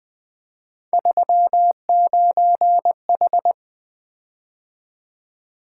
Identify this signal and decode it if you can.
Morse code, heard as a keyed tone, about 21 wpm, 700 Hz. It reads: 39H
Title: Three Letter-Number Combinations - 20wpm